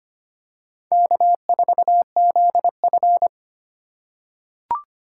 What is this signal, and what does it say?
Morse code, heard as a keyed tone, about 25 wpm, 700 Hz. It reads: K4ZF
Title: USA Call Signs - 2x Character Spacing - 25wpm